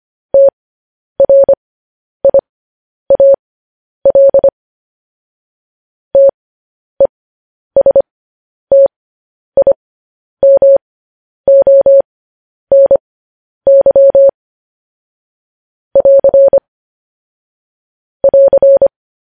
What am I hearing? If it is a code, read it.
Morse code, heard as a keyed tone, about 25 wpm, spaced as for 10 wpm, 550 Hz. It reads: TRIAL TESTIMONY <AR> <AR>
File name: podcast / News headlines-1x-25x10